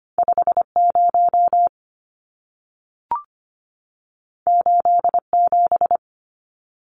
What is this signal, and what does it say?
Morse code, heard as a keyed tone, about 25 wpm, 700 Hz. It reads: 50 87
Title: Numbers - 2-Digit 25wpm